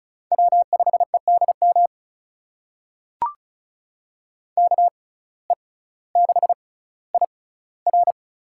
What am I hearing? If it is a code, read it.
Morse code, heard as a keyed tone, about 36 wpm, spaced as for 18 wpm, 700 Hz. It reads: W5EDM KE6IR
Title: USA Call Signs - 5 Characters - 6x Character Spacing - 35wpm